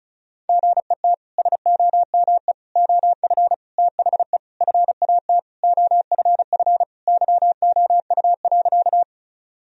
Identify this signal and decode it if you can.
Morse code, heard as a keyed tone, about 34 wpm, 700 Hz. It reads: GET SOME OF THE FAT OFF YOU.
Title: Sentences from Top 1000 Words - Review Part 4 - 35wpm